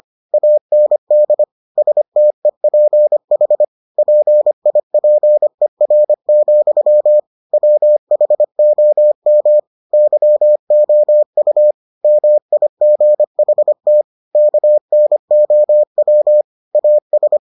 Morse code, heard as a keyed tone, about 25 wpm, 600 Hz. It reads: AND STEPH PIPER, WHOM YOU MIGHT KNOW AS